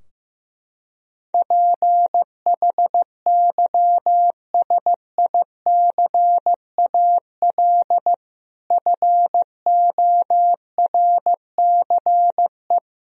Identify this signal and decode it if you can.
Morse code, heard as a keyed tone, about 14 wpm, 700 Hz. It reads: PHYSICAL FORCE
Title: Sentences from Top 1000 Words - Review Part 4 - 15wpm